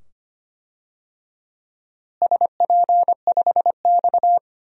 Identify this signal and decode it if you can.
Morse code, heard as a keyed tone, about 25 wpm, 700 Hz. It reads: SP5X
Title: DXCC-100 Call Signs - 4 Characters - Repeated 2x - 25wpm